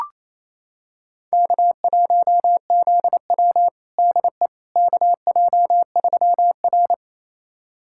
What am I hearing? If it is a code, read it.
Morse code, heard as a keyed tone, about 28 wpm, 700 Hz. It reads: K1ZW DE KJ3R
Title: USA Callsign DE Callsign - 4 Characters - 28wpm